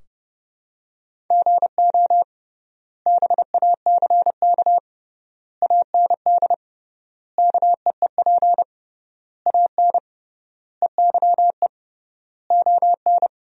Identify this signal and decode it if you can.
Morse code, heard as a keyed tone, about 32 wpm, 700 Hz. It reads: GO BACK AND KEEP AN EYE ON
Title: 3x Word-Spacing - Sentences from Top 300 Words - 30wpm